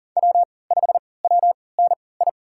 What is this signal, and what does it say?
Morse code, heard as a keyed tone, about 40 wpm, spaced as for 23 wpm, 700 Hz. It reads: W5WNI
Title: USA Call Signs - 5 Characters - 3x Character Spacing - 40wpm